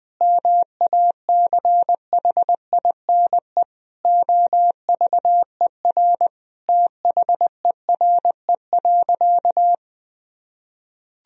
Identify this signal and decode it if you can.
Morse code, heard as a keyed tone, about 20 wpm, 700 Hz. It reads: MACHINE OVER THERE.
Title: Sentences from Top 1000 Words - Review Part 2 - 20wpm